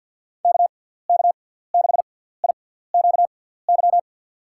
Morse code, heard as a keyed tone, about 48 wpm, spaced as for 17 wpm, 700 Hz. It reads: KK6IQY
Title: USA Call Signs - 6 Characters - 6x Character Spacing - 50wpm